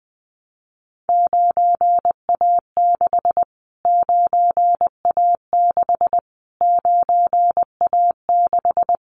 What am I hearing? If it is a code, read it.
Morse code, heard as a keyed tone, about 20 wpm, 700 Hz. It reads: 9A6 9A6 9A6
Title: Callsign Prefixes for Top 25 DXCC - Repeated 3x - 20wpm